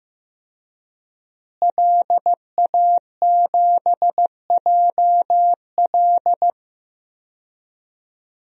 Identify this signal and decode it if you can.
Morse code, heard as a keyed tone, about 15 wpm, 700 Hz. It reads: LA7JL
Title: DXCC-100 Call Signs - 5 Characters - Repeated 3x - 15wpm